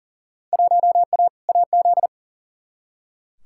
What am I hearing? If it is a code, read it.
Morse code, heard as a keyed tone, about 40 wpm, 700 Hz. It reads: 1A AZ
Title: Field Day Exchanges - 40wpm